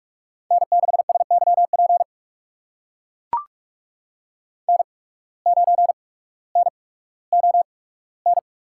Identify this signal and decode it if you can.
Morse code, heard as a keyed tone, about 44 wpm, spaced as for 19 wpm, 700 Hz. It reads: N6SYP N9NON
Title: USA Call Signs - 5 Characters - 8x Character Spacing - 45wpm